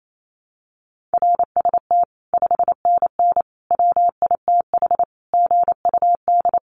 Morse code, heard as a keyed tone, about 28 wpm, 700 Hz. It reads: RST 5NN WITH GUD
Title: Common QSO Phrases 28wpm